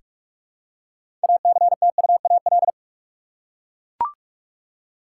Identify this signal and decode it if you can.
Morse code, heard as a keyed tone, about 43 wpm, 700 Hz. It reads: ACTUAL
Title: Six Letter Words - 45wpm